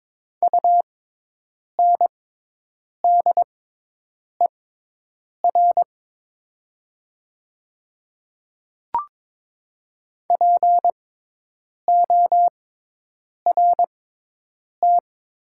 Morse code, heard as a keyed tone, about 22 wpm, spaced as for 8 wpm, 700 Hz. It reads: UNDER PORT
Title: 6x Character Spacing - Top 200 Words - 22wpm